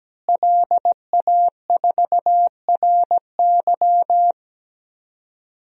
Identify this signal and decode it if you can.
Morse code, heard as a keyed tone, about 17 wpm, 700 Hz. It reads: LA4RY
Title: DXCC-100 Call Signs - 5 Characters - Repeated 3x - 17wpm